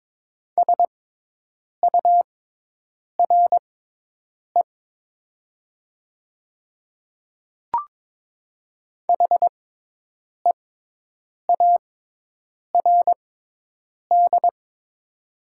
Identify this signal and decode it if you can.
Morse code, heard as a keyed tone, about 22 wpm, spaced as for 8 wpm, 700 Hz. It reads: SURE HEARD
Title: 6x Character Spacing - Top 500 Words - 22wpm